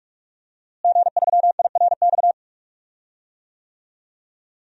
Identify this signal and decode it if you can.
Morse code, heard as a keyed tone, about 47 wpm, 700 Hz. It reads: G3IRX
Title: DXCC-100 Call Signs - 5 Characters - Repeated 3x - 45wpm